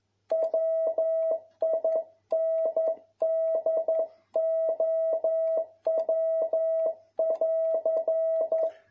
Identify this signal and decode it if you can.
Morse code, heard as a keyed tone, about 11 wpm, 650 Hz. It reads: WINDOW<AR>
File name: jh8cbh / 2021cw